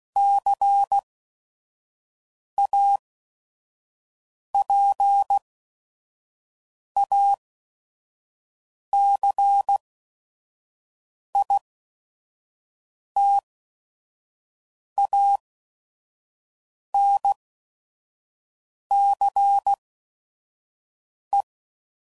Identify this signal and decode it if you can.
Morse code, heard as a keyed tone, about 16 wpm, spaced as for 5 wpm, 800 Hz. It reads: CAPACITANCE
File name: CW Code Groups @5WPM / Technical Subjects / Reactance